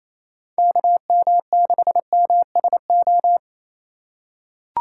Morse code, heard as a keyed tone, about 28 wpm, 700 Hz. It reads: KM6MSO
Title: USA Call Signs - 6 Characters - 4x Character Spacing - 28wpm